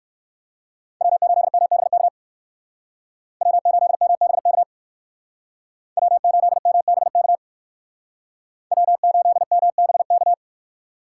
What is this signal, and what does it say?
Morse code, heard as a keyed tone, about 59 wpm, 700 Hz. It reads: W8MBK W8MBK W8MBK W8MBK